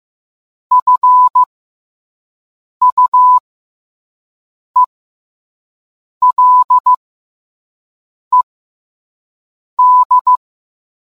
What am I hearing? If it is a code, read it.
Morse code, heard as a keyed tone, about 15 wpm, spaced as for 5 wpm, 1000 Hz. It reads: FUELED